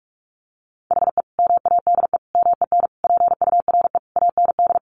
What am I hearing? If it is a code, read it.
Morse code, heard as a keyed tone, about 44 wpm, 700 Hz. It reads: HE MADE MEN PURE AND
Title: Sentences from Top 1500 Words 45wpm